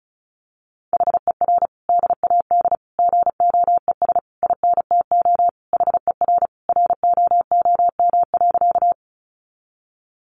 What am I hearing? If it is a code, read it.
Morse code, heard as a keyed tone, about 34 wpm, 700 Hz. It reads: HER DAD GOES INTO HER ROOM.
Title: Additional Easy Phrases 35wpm